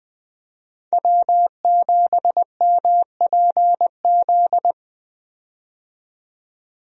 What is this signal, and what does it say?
Morse code, heard as a keyed tone, about 20 wpm, 700 Hz. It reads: W7MPZ
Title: USA Call Signs - 5 Characters - 20wpm